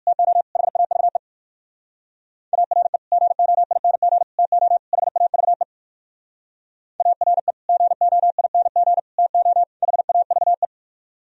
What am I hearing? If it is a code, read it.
Morse code, heard as a keyed tone, about 68 wpm, spaced as for 58 wpm, 700 Hz. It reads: TO HAVE ARE GOING TO HAVE ARE GOING TO HAVE